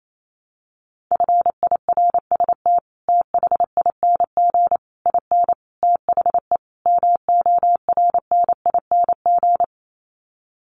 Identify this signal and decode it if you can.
Morse code, heard as a keyed tone, about 28 wpm, 700 Hz. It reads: FIRST THING IN THE MORNING
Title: Sets of 5 Words Encore2 28wpm